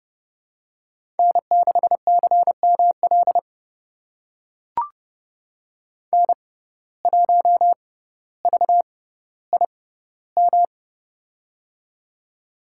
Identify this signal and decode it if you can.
Morse code, heard as a keyed tone, about 30 wpm, spaced as for 15 wpm, 700 Hz. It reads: N6CML N1VIM
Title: USA Call Signs - 5 Characters - 6x Character Spacing - 30wpm